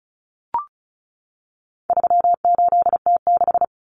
Morse code, heard as a keyed tone, about 34 wpm, 700 Hz. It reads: 38T6
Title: Common Cut Numbers - 4-Digit 35wpm